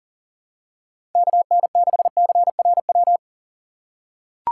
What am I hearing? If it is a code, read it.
Morse code, heard as a keyed tone, about 38 wpm, 700 Hz. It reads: KN6CRW
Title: USA Call Signs - 6 Characters - 6x Character Spacing - 40wpm